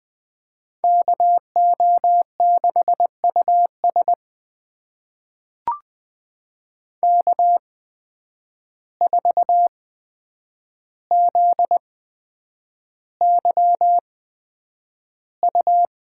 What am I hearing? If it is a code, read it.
Morse code, heard as a keyed tone, about 20 wpm, spaced as for 9 wpm, 700 Hz. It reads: KO6US K4ZYU
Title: USA Call Signs - 5 Characters - 8x Character Spacing - 20wpm